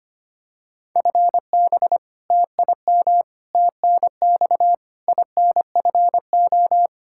Morse code, heard as a keyed tone, about 25 wpm, 700 Hz. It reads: FB TIM TNX INFO
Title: K1USN SST QSO Practice 25wpm